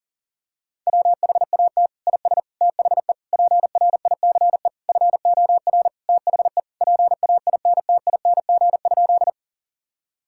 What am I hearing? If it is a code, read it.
Morse code, heard as a keyed tone, about 40 wpm, 700 Hz. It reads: WHAT IS THE PRICE FOR THE PAINTING?
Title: Sentences from Top 1000 Words - Review Part 6 - 40wpm